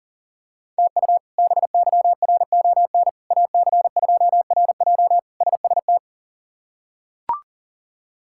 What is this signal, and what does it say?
Morse code, heard as a keyed tone, about 40 wpm, 700 Hz. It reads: TU BYRON AC2RJ SST
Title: K1USN SST QSO Practice - 40wpm